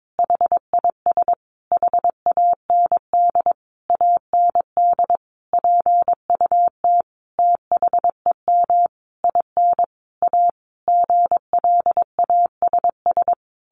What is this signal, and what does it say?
Morse code, heard as a keyed tone, about 22 wpm, 700 Hz. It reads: HIS HAND AND PUT THEM IN A GLASS
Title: Sentences from Top 1300 Words 22wpm